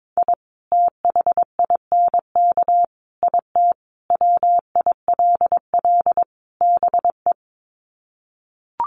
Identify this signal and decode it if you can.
Morse code, heard as a keyed tone, about 22 wpm, 700 Hz. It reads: I THINK IT WILL BE
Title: Sets of 5 Words Encore2 22wpm